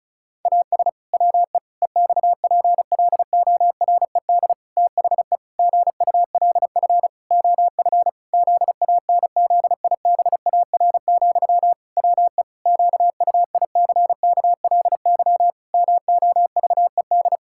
Morse code, heard as a keyed tone, about 34 wpm, 700 Hz. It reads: AS WE EXPLORED THE GULF OF ZANZIBAR, WE QUICKLY MOVED
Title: Pangram Challenge v2 - 35wpm